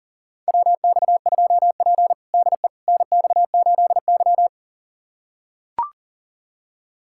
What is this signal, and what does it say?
Morse code, heard as a keyed tone, about 40 wpm, 700 Hz. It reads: WX2P DE NX8Y
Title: USA Callsign DE Callsign - 4 Characters - 40wpm